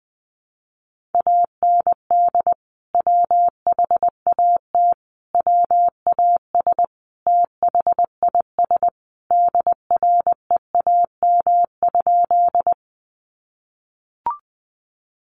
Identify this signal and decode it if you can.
Morse code, heard as a keyed tone, about 20 wpm, 700 Hz. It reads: AND WHAT WAS THIS DREAM?
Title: Sentences from Top 1500 Words 20wpm